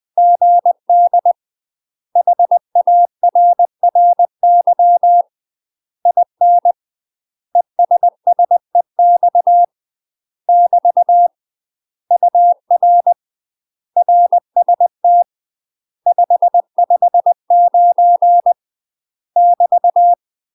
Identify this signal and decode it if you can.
Morse code, heard as a keyed tone, about 20 wpm, 700 Hz. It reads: GD HARRY IN ESSEX <BT> UR RST 559 <BT>
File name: UK MASTER QSOs TRAINING / 20WPM / UK MASTER QSOs 0068